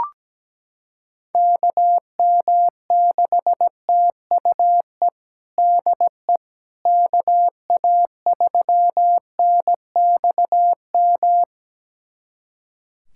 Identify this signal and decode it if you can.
Morse code, heard as a keyed tone, about 17 wpm, 700 Hz. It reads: KM6TUE DE KA3NXM
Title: USA Callsign DE Callsign 17wpm